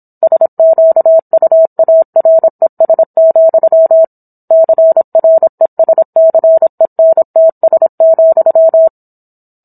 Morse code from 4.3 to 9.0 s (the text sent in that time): CRESCENTS,